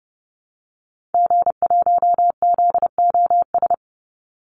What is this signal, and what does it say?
Morse code, heard as a keyed tone, about 30 wpm, 700 Hz. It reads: G1ZOS
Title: Call Signs - 30wpm